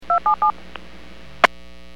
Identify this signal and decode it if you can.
DTMF, heard as telephone touch tones, tones of about 110 ms. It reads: 277
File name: WPS / 00755 21-59 WPS 451.25 WPS